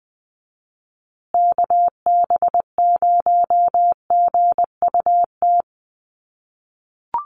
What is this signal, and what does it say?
Morse code, heard as a keyed tone, about 20 wpm, 700 Hz. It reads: KB0GUT
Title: USA Call Signs Repeated 2x 20wpm